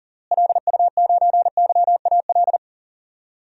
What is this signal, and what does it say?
Morse code, heard as a keyed tone, about 37 wpm, 700 Hz. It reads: LU9YAL
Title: DXCC-100 Call Signs - 6 Characters - Repeated 3x - 40wpm